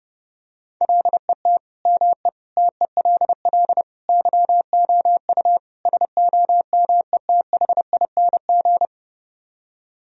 Morse code, heard as a keyed tone, about 29 wpm, 700 Hz. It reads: LET ME TELL YOU SOMETHING